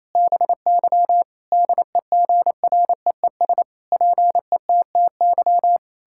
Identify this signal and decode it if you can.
Morse code, heard as a keyed tone, about 28 wpm, 700 Hz. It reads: BY DEGREES PETTY
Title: Pangram Challenge v2 - 28wpm